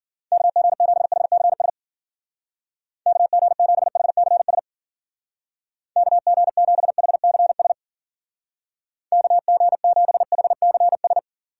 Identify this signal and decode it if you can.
Morse code, heard as a keyed tone, about 49 wpm, 700 Hz. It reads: KG7HCS KG7HCS KG7HCS KG7HCS
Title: Speed Racing - USA 2x3 Call Signs 40wpm